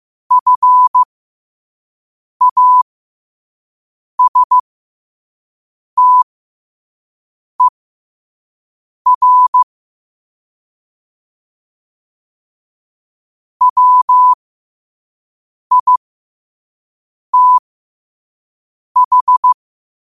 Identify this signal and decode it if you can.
Morse code, heard as a keyed tone, about 15 wpm, spaced as for 5 wpm, 1000 Hz. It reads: FASTER WITH